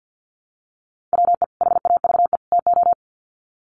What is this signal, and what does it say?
Morse code, heard as a keyed tone, about 50 wpm, 700 Hz. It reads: WE HAVE TO
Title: Sets of 3 Words 50wpm